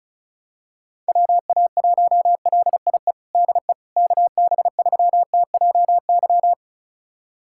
Morse code, heard as a keyed tone, about 34 wpm, 700 Hz. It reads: WA1LIE DE KB3TJY
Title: USA Callsign DE Callsign - 6 Characters - 35wpm